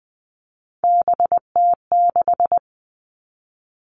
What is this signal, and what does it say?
Morse code, heard as a keyed tone, about 20 wpm, 700 Hz. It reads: BT6
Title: All Cut Numbers - 3-Digit 20wpm